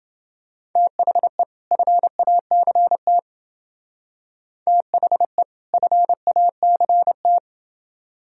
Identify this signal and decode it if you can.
Morse code, heard as a keyed tone, about 28 wpm, 700 Hz. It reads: THE FACT THE FACT